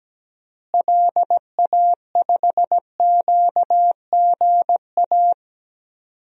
Morse code, heard as a keyed tone, about 17 wpm, 700 Hz. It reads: LA5QGA
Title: DXCC-100 Call Signs - 6 Characters - Repeated 2x - 17wpm